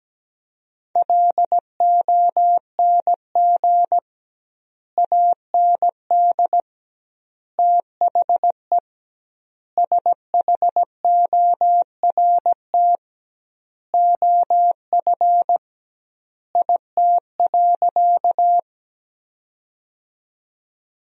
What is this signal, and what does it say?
Morse code, heard as a keyed tone, about 17 wpm, 700 Hz. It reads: LONG AND THE SHORT OF IT.
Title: 2x Word-Spacing - Sentences from Top 400 Words - 17wpm